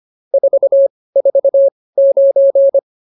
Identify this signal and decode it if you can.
Morse code, heard as a keyed tone, about 24 wpm, spaced as for 18 wpm, 550 Hz. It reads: 449
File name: POTA206 18